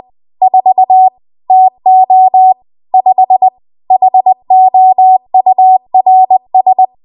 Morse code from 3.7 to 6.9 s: HOURS